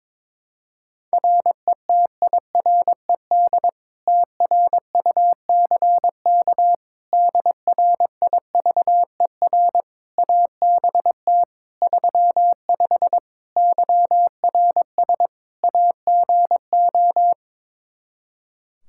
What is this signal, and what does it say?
Morse code, heard as a keyed tone, about 21 wpm, 700 Hz. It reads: RETIRED TRUCK DRIVER ABT 35 YRS AGO
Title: Practice QSO - Exchange 3 - More Info - 22wpm